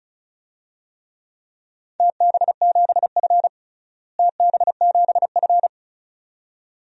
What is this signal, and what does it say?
Morse code, heard as a keyed tone, about 36 wpm, 700 Hz. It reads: TB7F TB7F
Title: DXCC-100 Call Signs - 4 Characters - Repeated 2x - 35wpm